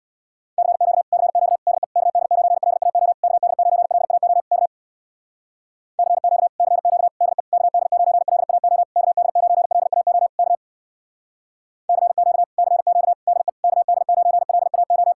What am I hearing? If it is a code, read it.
Morse code, heard as a keyed tone, about 65 wpm, 700 Hz. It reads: CQ CQ DE KD9LAO KD9LAO K CQ CQ DE KD9LAO KD9LAO K CQ CQ DE KD9LAO